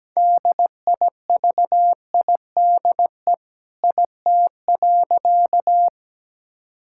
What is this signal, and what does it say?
Morse code, heard as a keyed tone, about 17 wpm, 700 Hz. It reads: DIVIDE IT.
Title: Sentences from Top 1600 Words 17wpm